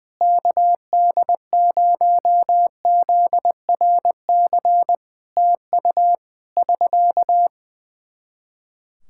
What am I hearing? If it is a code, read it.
Morse code, heard as a keyed tone, about 19 wpm, 700 Hz. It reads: KD0ZRC TU <SK>
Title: Practice QSO - Exchange 4 - Ending - 20wpm